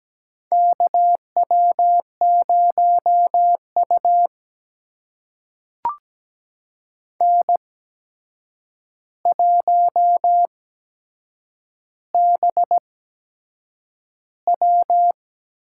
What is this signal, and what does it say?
Morse code, heard as a keyed tone, about 17 wpm, spaced as for 7 wpm, 700 Hz. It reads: KW0U N1BW
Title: USA Call Signs - 4 Characters - 8x Character Spacing - 17wpm